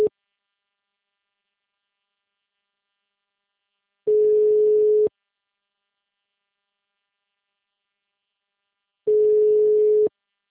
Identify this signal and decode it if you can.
Ringback tone (Europe, 425 Hz).
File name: out- -2005- - - .123